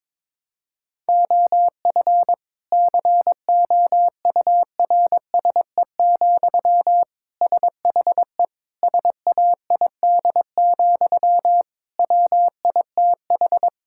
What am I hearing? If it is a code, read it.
Morse code, heard as a keyed tone, about 22 wpm, 700 Hz. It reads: OF COURSE, SHE SAID, WITH
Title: Sentences from Top 1600 Words 22wpm